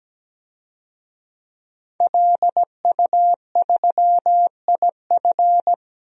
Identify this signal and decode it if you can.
Morse code, heard as a keyed tone, about 17 wpm, 700 Hz. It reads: LU3IF
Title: DXCC-100 Call Signs - 5 Characters - 17wpm